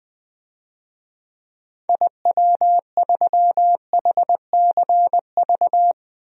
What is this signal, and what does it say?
Morse code, heard as a keyed tone, about 20 wpm, 700 Hz. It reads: IW3HCV
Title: DXCC-100 Call Signs - 6 Characters - Repeated 2x - 20wpm